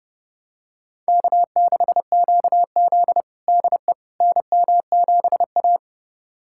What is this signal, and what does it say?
Morse code, heard as a keyed tone, about 30 wpm, 700 Hz. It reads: K6QZ DE NM7A
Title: USA Callsign DE Callsign - 4 Characters - 30wpm